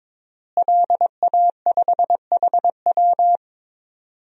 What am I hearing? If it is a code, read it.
Morse code, heard as a keyed tone, about 22 wpm, 700 Hz. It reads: LA5HW
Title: DXCC-100 Call Signs - 5 Characters - Repeated 3x - 22wpm